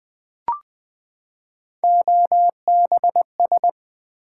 Morse code, heard as a keyed tone, about 20 wpm, 700 Hz. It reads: OBS
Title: Build A Word - 3 Letters At A Time - 9-Letter Words - 20wpm